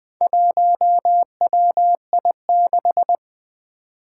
Morse code, heard as a keyed tone, about 20 wpm, 700 Hz. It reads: 1WI6
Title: Four Challenging Letter-Number Combinations 20wpm